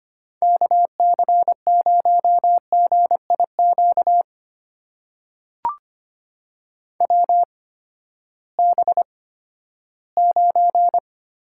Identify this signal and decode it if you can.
Morse code, heard as a keyed tone, about 25 wpm, spaced as for 14 wpm, 700 Hz. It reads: KC0GIQ WB9
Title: USA Call Signs - 6 Characters - 8x Character Spacing - 25wpm